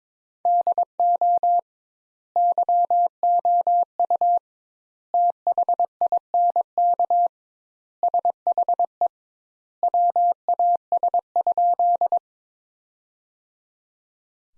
Morse code, heard as a keyed tone, about 21 wpm, 700 Hz. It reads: DO YOU THINK SHE WAS?
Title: Sentences Top 200 Words - 22wpm